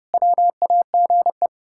Morse code, heard as a keyed tone, about 30 wpm, 700 Hz. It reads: WAGE
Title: Pangram Challenge v2 - 30wpm